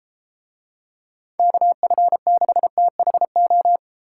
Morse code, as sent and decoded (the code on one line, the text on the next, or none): -.- ..-. -.... - .... ---
KF6THO